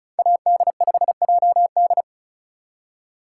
Morse code, heard as a keyed tone, about 36 wpm, 700 Hz. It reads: AD5JD
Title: USA Call Signs - 5 Characters - 4x Character Spacing - 35wpm